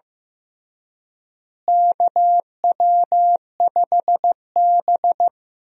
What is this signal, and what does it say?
Morse code, heard as a keyed tone, about 15 wpm, 700 Hz. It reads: KW5B
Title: USA Call Signs - 4 Characters - Repeated 3x - 15wpm